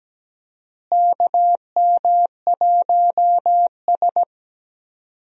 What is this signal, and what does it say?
Morse code, heard as a keyed tone, about 17 wpm, 700 Hz. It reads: KM1S
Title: USA Call Signs - 4 Characters - Repeated 3x - 17wpm